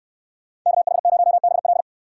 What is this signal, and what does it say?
Morse code, heard as a keyed tone, about 69 wpm, 700 Hz. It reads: KH0BL